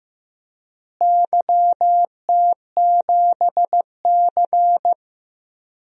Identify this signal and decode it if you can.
Morse code, heard as a keyed tone, about 15 wpm, 700 Hz. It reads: YT7C
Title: DXCC-100 Call Signs - 4 Characters - Repeated 3x - 15wpm